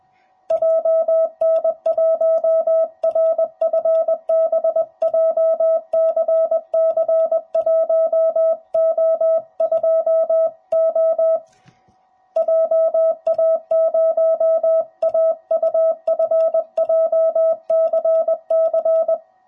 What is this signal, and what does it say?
Morse code, heard as a keyed tone, about 20 wpm, 650 Hz. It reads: JN1RFB JCC1O2O JA0AUF JCC
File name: jh8cbh / 2021cw / cw05b